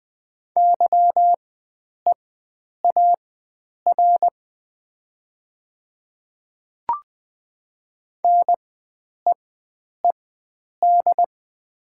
Morse code, heard as a keyed tone, about 20 wpm, spaced as for 9 wpm, 690 Hz. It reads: YEAR NEED
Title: 4x Character Spacing - Top 300 Words - 20wpm